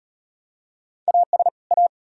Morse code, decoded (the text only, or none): AS A